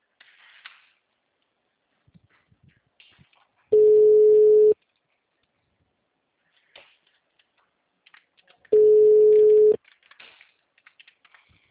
Ringback tone (Europe, 425 Hz).
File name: out- -2012- - - .168